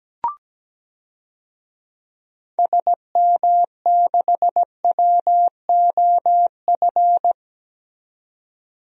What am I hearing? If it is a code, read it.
Morse code, heard as a keyed tone, about 17 wpm, 700 Hz. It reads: SM6WOF
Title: DXCC-100 Call Signs - 6 Characters - 17wpm